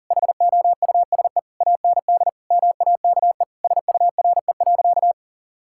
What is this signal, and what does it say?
Morse code, heard as a keyed tone, about 40 wpm, 700 Hz. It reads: HOUSE AND MAKE SURE.
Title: Sentences from Top 1000 Words - Review Part 5 - 40wpm